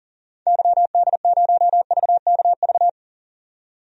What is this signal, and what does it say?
Morse code, heard as a keyed tone, about 40 wpm, 700 Hz. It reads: YD0VKV
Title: DXCC-100 Call Signs - 6 Characters - 40wpm